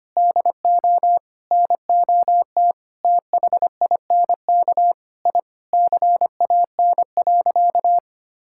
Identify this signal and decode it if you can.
Morse code, heard as a keyed tone, about 25 wpm, 700 Hz. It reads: DO NOT THINK I CAN.
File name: practice / 25wpm / Sentences with 4-Word Phrases - 25wpm